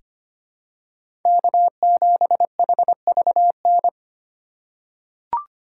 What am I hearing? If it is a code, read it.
Morse code, heard as a keyed tone, about 25 wpm, 700 Hz. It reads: K7HVN
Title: USA Call Signs - 5 Characters - 2x Character Spacing - 25wpm